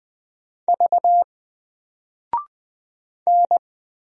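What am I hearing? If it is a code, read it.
Morse code, heard as a keyed tone, about 20 wpm, 700 Hz. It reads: V N
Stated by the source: Single letter and number Morse drill at 20 wpm, from N4LCD.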